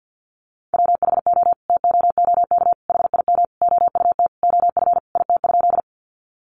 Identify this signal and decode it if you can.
Morse code, heard as a keyed tone, about 50 wpm, 700 Hz. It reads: WHO TOOK HIM OUT OF IT?
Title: Sentences from Top 300 Words 50wpm